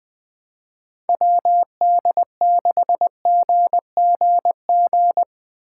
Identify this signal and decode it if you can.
Morse code, heard as a keyed tone, about 20 wpm, 700 Hz. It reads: WD6GGG
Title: USA Call Signs - 6 Characters - 4x Character Spacing - 20wpm